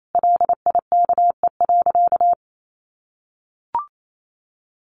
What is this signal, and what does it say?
Morse code, heard as a keyed tone, about 28 wpm, 700 Hz. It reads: LIKE.
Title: Sentences from Top 1100 Words 28wpm